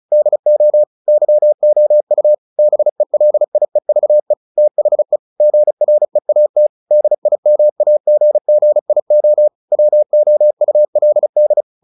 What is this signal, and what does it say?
Morse code, heard as a keyed tone, about 34 wpm, 600 Hz. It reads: DO YOU BELIEVE THE GREAT DIMAGGIO WOULD
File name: the old man and the sea / the old man and the sea 1 35wpm